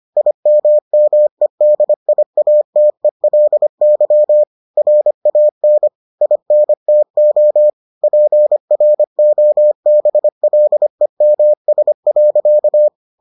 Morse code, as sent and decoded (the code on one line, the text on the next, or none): .. -- -- . -.. .. .- - . .-.. -.-- / .-. .- -. / .. -. - --- / .--. .-. --- -... .-.. -- ... .-.-.-
IMMEDIATELY RAN INTO PROBLEMS.